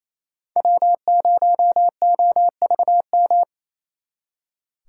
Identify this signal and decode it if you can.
Morse code, heard as a keyed tone, about 36 wpm, 700 Hz. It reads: W0OVM